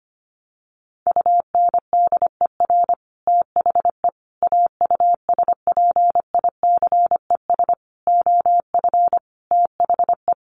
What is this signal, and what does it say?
Morse code, heard as a keyed tone, about 25 wpm, 700 Hz. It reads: UNDER THE AUSPICES OF THE
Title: Sets of 5 Words Encore2 25wpm